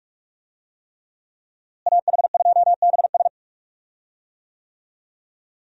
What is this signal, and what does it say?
Morse code, heard as a keyed tone, about 46 wpm, 700 Hz. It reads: AH2BS
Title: USA Call Signs - 5 Characters - Repeated 3x - 45wpm